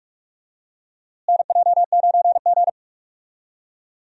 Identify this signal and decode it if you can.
Morse code, heard as a keyed tone, about 42 wpm, 700 Hz. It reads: NJ9G
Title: USA Call Signs - 4 Characters - Repeated 2x - 45wpm